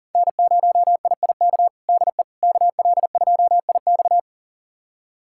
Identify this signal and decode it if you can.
Morse code, heard as a keyed tone, about 40 wpm, 700 Hz. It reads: N0IIK DE KL2IX
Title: USA Callsign DE Callsign - 5 Characters - 40wpm